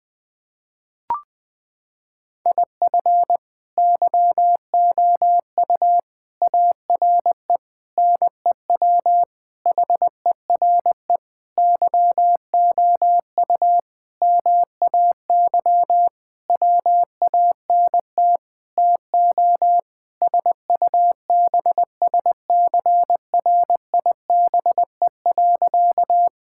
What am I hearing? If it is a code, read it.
Morse code, heard as a keyed tone, about 20 wpm, 700 Hz. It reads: IF YOU ARE NEW HERE YOU MAY WANT TO SUBSCRIBE.
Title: Sentences with 8-Word Phrases - 20wpm